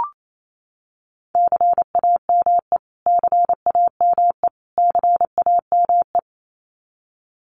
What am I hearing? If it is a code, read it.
Morse code, heard as a keyed tone, about 28 wpm, 700 Hz. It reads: CAME CAME CAME
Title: Top 200 Words Repeated 28wpm